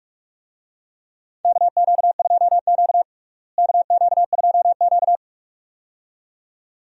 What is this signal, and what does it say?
Morse code, heard as a keyed tone, about 46 wpm, 700 Hz. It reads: KQ2Q KQ2Q
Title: USA Call Signs - 4 Characters - Repeated 2x - 45wpm